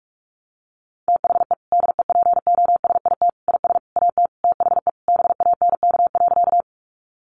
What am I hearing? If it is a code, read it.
Morse code, heard as a keyed tone, about 44 wpm, 700 Hz. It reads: THE DEPOSIT IS AT THE BANK.
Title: Sentences from Top 1300 Words 45wpm